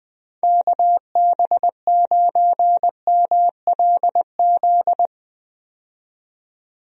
Morse code, heard as a keyed tone, about 21 wpm, 700 Hz. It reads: KB9MLZ
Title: USA Call Signs - 6 Characters - 20wpm